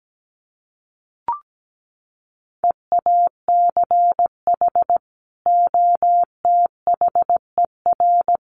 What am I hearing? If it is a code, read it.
Morse code, heard as a keyed tone, about 17 wpm, 700 Hz. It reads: EACH OTHER
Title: Sets of 2 Words 17wpm